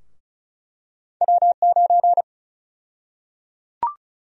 Morse code, heard as a keyed tone, about 34 wpm, 700 Hz. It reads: W9
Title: USA Call Sign Prefixes - Repeated 2x - 35wpm